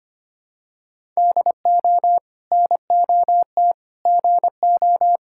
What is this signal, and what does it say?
Morse code, heard as a keyed tone, about 25 wpm, 700 Hz. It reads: DO NOT GO
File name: practice / 25wpm / Sentences with 2-Word Phrases - 25wpm